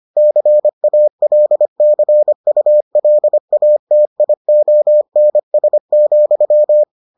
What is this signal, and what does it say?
Morse code, heard as a keyed tone, about 25 wpm, 600 Hz. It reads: CALCULATIONS,